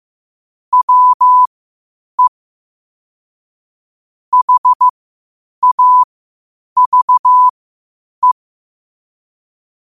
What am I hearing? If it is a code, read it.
Morse code, heard as a keyed tone, about 15 wpm, spaced as for 8 wpm, 1000 Hz. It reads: WE HAVE
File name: qotdfiles / 8wpm